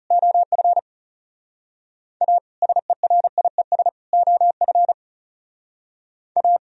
Morse code, heard as a keyed tone, about 38 wpm, 700 Hz. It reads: OF A SERIES OF A